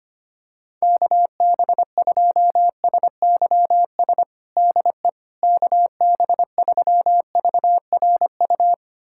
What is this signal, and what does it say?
Morse code, heard as a keyed tone, about 25 wpm, 700 Hz. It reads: KB2SYS DE KB3VRU
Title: USA Callsign DE Callsign - 6 Characters - 25wpm